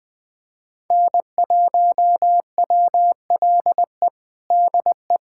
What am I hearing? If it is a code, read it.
Morse code, heard as a keyed tone, about 20 wpm, 700 Hz. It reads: N1WLE DE
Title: USA Callsign DE Callsign - 5 Characters - 20wpm